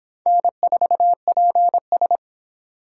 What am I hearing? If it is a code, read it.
Morse code, heard as a keyed tone, about 26 wpm, 700 Hz. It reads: N4PS